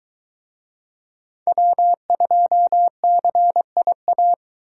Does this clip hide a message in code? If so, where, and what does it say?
Yes: Morse code from 1.5 to 4.3 s, heard as a keyed tone, about 23 wpm, 700 Hz. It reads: W2CIA